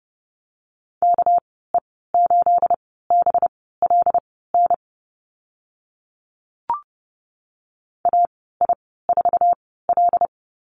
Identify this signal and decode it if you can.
Morse code, heard as a keyed tone, about 32 wpm, spaced as for 17 wpm, 700 Hz. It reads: KE8BLN AI4L
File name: Lesson 37g - Letter x - Call Signs 30wpm